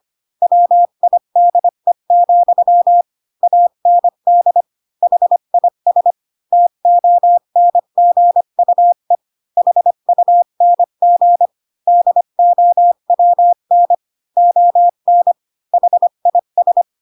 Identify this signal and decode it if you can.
Morse code, heard as a keyed tone, about 25 wpm, 700 Hz. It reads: WIDE, AND HIS TONGUE HUNG DOWN ON HIS
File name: Pinocchio / Pinocchio ch1 20 25 ik1hge